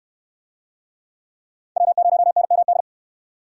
Morse code, heard as a keyed tone, about 62 wpm, 700 Hz. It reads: W0NMD